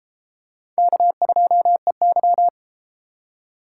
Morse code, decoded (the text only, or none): K2EY